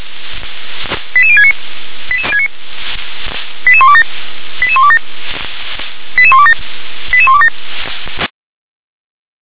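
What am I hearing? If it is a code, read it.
Five-tone selective call (ZVEI), heard as quick runs of five tones, tones of about 70 ms. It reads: 88069 88127 88127 88127 88127
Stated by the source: ZVEI five-tone selective call received with an SDR.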